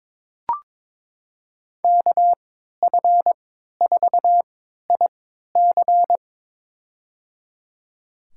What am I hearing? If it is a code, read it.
Morse code, heard as a keyed tone, about 21 wpm, spaced as for 12 wpm, 700 Hz. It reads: KF4IC
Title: USA Call Signs - 5 Characters - 3x Character Spacing - 22wpm